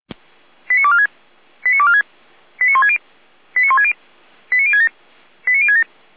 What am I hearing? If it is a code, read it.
Five-tone selective call (ZVEI), heard as quick runs of five tones, tones of about 70 ms. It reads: 89246 89246 89160 89160 89067 89067
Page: audio.f85360.feuerwehr-orlamuende.org